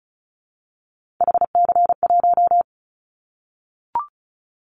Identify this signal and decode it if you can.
Morse code, heard as a keyed tone, about 36 wpm, 700 Hz. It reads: HC1
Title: Callsign Prefixes for Top 100 DXCC - 35wpm